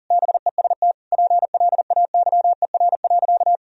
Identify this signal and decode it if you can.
Morse code, heard as a keyed tone, about 40 wpm, 700 Hz. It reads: BEST PLAYER.
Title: Sentences from Top 1000 Words - Review Part 4 - 40wpm